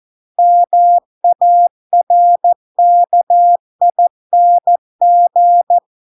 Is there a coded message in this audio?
Yes: Morse code from 0.4 to 5.8 s, heard as a keyed tone, about 14 wpm, 700 Hz. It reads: MARKING